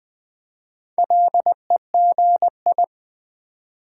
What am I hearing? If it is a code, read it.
Morse code, heard as a keyed tone, about 20 wpm, 700 Hz. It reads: LEGI
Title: Build A Word - 4 Letters At A Time - 11-Letter Words - 20wpm